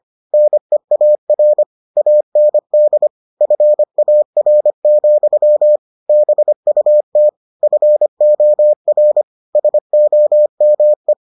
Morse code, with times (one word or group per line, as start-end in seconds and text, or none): 0.3-1.6 NEAR
2.0-3.1 AND
3.4-5.8 FAR,
6.1-7.3 BUT
7.6-9.2 FOR
9.6-11.1 SOME